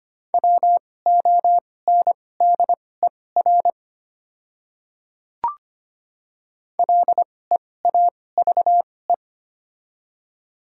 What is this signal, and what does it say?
Morse code, heard as a keyed tone, about 25 wpm, spaced as for 18 wpm, 700 Hz. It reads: WONDER LEAVE
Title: 2x Character Spacing - Top 500 Words - 25wpm